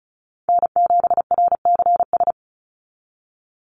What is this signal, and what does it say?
Morse code, heard as a keyed tone, about 34 wpm, 700 Hz. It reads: N7RCS
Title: Call Signs Repeated 3x - 35wpm